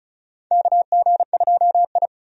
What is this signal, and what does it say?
Morse code, heard as a keyed tone, about 36 wpm, 700 Hz. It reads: KG2I
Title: USA Call Signs - 4 Characters - 35wpm